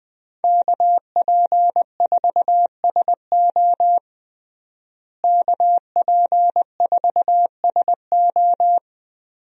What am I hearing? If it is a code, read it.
Morse code, heard as a keyed tone, about 20 wpm, 700 Hz. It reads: KP4SO KP4SO
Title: DXCC-100 Call Signs - 5 Characters - Repeated 2x - 20wpm